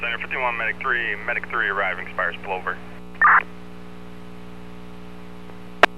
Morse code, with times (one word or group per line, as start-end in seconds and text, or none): none